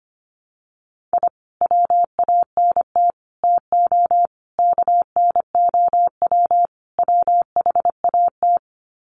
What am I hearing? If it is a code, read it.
Morse code, heard as a keyed tone, about 25 wpm, 700 Hz. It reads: I WANT TO KNOW WHAT
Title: Sets of 5 Words Encore2 25wpm